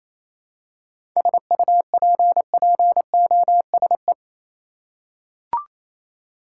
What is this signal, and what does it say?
Morse code, heard as a keyed tone, about 28 wpm, 700 Hz. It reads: SUPPOSE
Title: Seven Letter Words - 28wpm